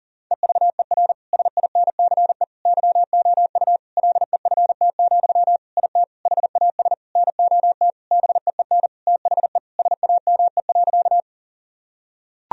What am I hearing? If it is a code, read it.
Morse code, heard as a keyed tone, about 40 wpm, 700 Hz. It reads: EVER SINCE YOU LEFT, IT HAS NOT BEEN THE SAME.
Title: Sentences from Top 1000 Words - Review Part 1 - 40wpm